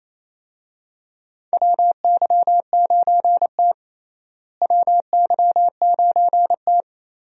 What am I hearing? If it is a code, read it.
Morse code, heard as a keyed tone, about 28 wpm, 700 Hz. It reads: WY9T WY9T
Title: USA Call Signs - 4 Characters - Repeated 3x - 28wpm